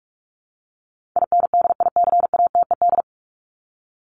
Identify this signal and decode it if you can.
Morse code, heard as a keyed tone, about 44 wpm, 700 Hz. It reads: INDICATED
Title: ED Suffix - Top 100 Words with Base Words 45wpm